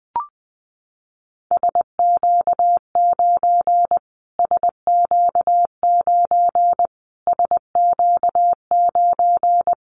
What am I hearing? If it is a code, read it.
Morse code, heard as a keyed tone, about 20 wpm, 700 Hz. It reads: SQ9 SQ9 SQ9
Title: Callsign Prefixes for Top 25 DXCC - Repeated 3x - 20wpm